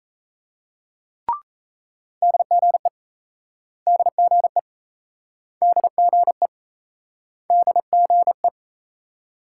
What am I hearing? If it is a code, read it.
Morse code, heard as a keyed tone, about 34 wpm, 700 Hz. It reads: DGE DGE DGE DGE
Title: Speed-Racing Multi-letter Phonemes - 28wpm